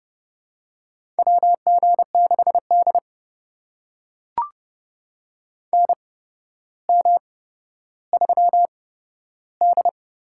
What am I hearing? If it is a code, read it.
Morse code, heard as a keyed tone, about 30 wpm, spaced as for 13 wpm, 700 Hz. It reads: WG6D NM3D
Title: USA Call Signs - 4 Characters - 8x Character Spacing - 30wpm